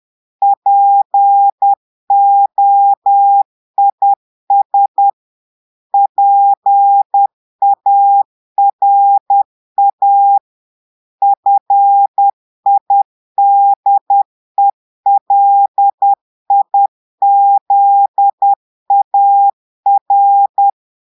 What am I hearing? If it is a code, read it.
Morse code, heard as a keyed tone, about 10 wpm, 800 Hz. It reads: POIS PARA FIDELIZAR
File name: portugais texte en clair 2 vitesse 10 mots minutes